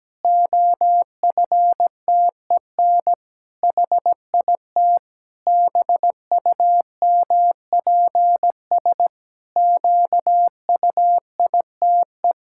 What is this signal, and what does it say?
Morse code, heard as a keyed tone, about 17 wpm, 700 Hz. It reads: OFTEN HIT BUMPS QUITE
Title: Pangram Challenge v2 - 17wpm